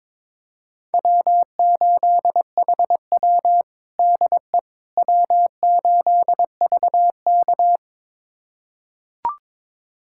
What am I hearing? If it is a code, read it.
Morse code, heard as a keyed tone, about 22 wpm, 700 Hz. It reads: W8HW DE W8VK
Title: USA Callsign DE Callsign - 4 Characters - 22wpm